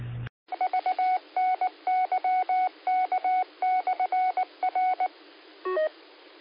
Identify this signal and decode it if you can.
Morse code, heard as a keyed tone, about 19 wpm, 740 Hz. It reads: VNYK/R
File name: qsorec SimplexLogic